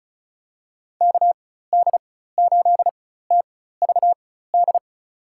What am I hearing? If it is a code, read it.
Morse code, heard as a keyed tone, about 34 wpm, spaced as for 16 wpm, 700 Hz. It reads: KD8TVD
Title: USA Call Signs - 6 Characters - 4x Character Spacing - 35wpm